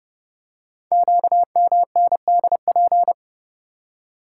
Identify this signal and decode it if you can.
Morse code, heard as a keyed tone, about 30 wpm, 700 Hz. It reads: QMNDP
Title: Code Groups - 5 Letters - 30wpm